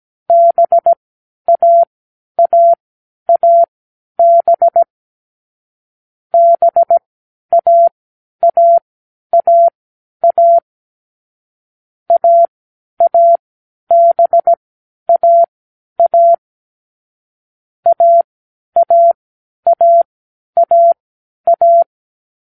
Morse code, heard as a keyed tone, about 17 wpm, spaced as for 10 wpm, 680 Hz. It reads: BAAAB BAAAA AABAA AAAAA